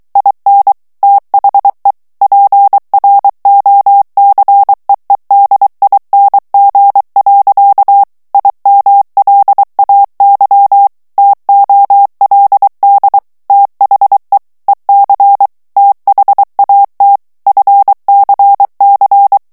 Morse code, heard as a keyed tone, about 23 wpm, 800 Hz. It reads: IN THE PROCEEDING. IMLAY TOLD THE EC THAT FCC
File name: ANGLAIS texte clair 3 vitesse 22 mots minute